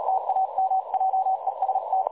Morse code, heard as a keyed tone, about 37 wpm, 760 Hz, with noise. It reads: TAM JEST
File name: dlazaawansowanych / nagraniazpasma / websdr recording 3565.0kHz